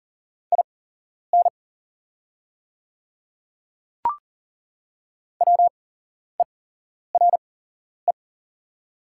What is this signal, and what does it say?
Morse code, heard as a keyed tone, about 40 wpm, spaced as for 11 wpm, 690 Hz. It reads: IN WERE
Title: 8x Character Spacing - Top 200 Words - 40wpm